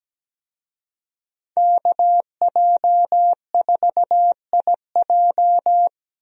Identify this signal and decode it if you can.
Morse code, heard as a keyed tone, about 17 wpm, 700 Hz. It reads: KJ4IJ
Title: USA Call Signs - 5 Characters - 17wpm